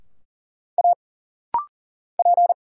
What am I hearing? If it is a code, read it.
Morse code, heard as a keyed tone, about 40 wpm, 700 Hz. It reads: A P
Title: Single Letter-Number - Rapid-Fire 40wpm